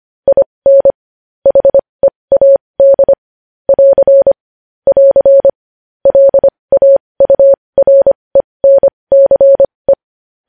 Morse code, heard as a keyed tone, about 25 wpm, spaced as for 20 wpm, 550 Hz. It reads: IN HEAD <AR> <AR> LAURENCE
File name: podcast / News headlines-1x-25x20